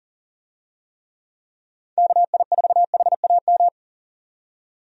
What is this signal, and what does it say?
Morse code, heard as a keyed tone, about 40 wpm, 700 Hz. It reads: KI4HAM
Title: USA Call Signs - 6 Characters - 40wpm